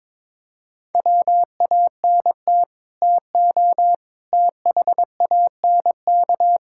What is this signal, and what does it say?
Morse code, heard as a keyed tone, about 22 wpm, 700 Hz. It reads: WANT TO THANK
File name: practice / 22wpm / Sets of 7 Words - 22wpm